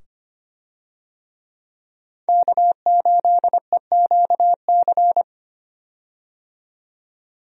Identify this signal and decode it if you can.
Morse code, heard as a keyed tone, about 25 wpm, 700 Hz. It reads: K8EQC
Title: USA Call Signs - 5 Characters - 25wpm